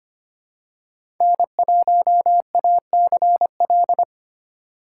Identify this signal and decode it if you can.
Morse code, heard as a keyed tone, about 25 wpm, 700 Hz. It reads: N1ACL
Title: USA Call Signs - 5 Characters - 2x Character Spacing - 25wpm